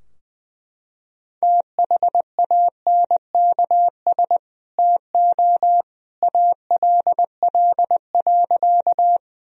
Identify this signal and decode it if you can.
Morse code, heard as a keyed tone, about 19 wpm, 700 Hz. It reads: THANKS TO ALL.
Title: Sentences from Top 1000 Words - Review Part 3 - 20wpm